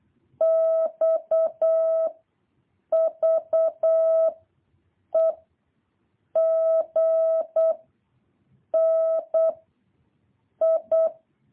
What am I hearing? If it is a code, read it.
Morse code, heard as a keyed tone, about 8 wpm, 660 Hz. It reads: X V E G N I
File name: jh8cbh / 2023cw